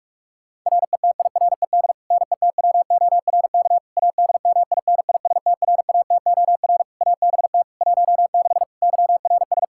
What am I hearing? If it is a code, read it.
Morse code, heard as a keyed tone, about 42 wpm, 700 Hz. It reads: RETIRED NETWORK ADMINISTRATOR ABT 16 YRS
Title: Practice QSO - Exchange 3 - More Info - 45wpm